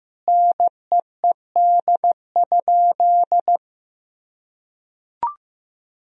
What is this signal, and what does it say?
Morse code, heard as a keyed tone, about 15 wpm, 700 Hz. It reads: NEED?
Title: Sentences from Top 1000 Words - Review Part 4 - 15wpm